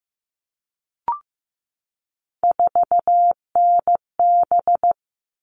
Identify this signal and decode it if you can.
Morse code, heard as a keyed tone, about 15 wpm, 700 Hz. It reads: 4NB
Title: All Cut Numbers - 3-Digit 15wpm